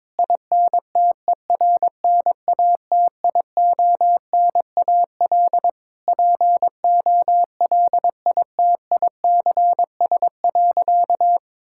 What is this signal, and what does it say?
Morse code, heard as a keyed tone, about 22 wpm, 700 Hz. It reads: INTERNATIONAL POLITICS.
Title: Sentences from Top 1000 Words - Review Part 6 - 22wpm